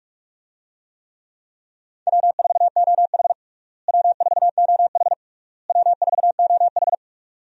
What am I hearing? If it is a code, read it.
Morse code, heard as a keyed tone, about 47 wpm, 700 Hz. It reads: W4OH W4OH W4OH
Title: USA Call Signs - 4 Characters - Repeated 3x - 45wpm